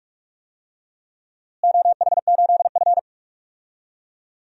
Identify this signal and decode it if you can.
Morse code, heard as a keyed tone, about 46 wpm, 700 Hz. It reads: OH8F
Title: DXCC-100 Call Signs - 4 Characters - Repeated 3x - 45wpm